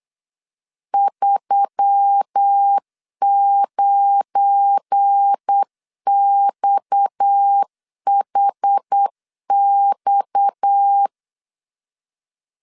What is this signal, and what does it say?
Morse code, heard as a keyed tone, about 8 wpm, 800 Hz. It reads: 39XHX
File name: LEZIONI / 9 2 8wpm 3437